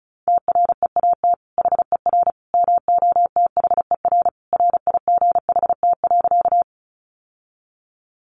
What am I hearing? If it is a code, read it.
Morse code, heard as a keyed tone, about 34 wpm, 700 Hz. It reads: TREAT HER MOTHER RIGHT.
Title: Sentences from Top 1300 Words 35wpm